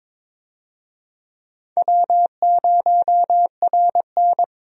Morse code, heard as a keyed tone, about 22 wpm, 700 Hz. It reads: W0RN
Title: USA Call Signs - 4 Characters - Repeated 3x - 22wpm